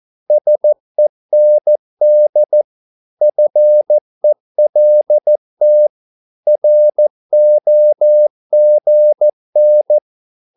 Morse code, heard as a keyed tone, about 14 wpm, 600 Hz. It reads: SEND FELT ROGN
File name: Korte ord DK 14wpm